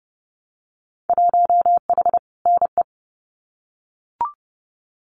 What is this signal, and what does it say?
Morse code, heard as a keyed tone, about 30 wpm, 700 Hz. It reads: 1H NE
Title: Winter Field Day Exchanges Encore 30wpm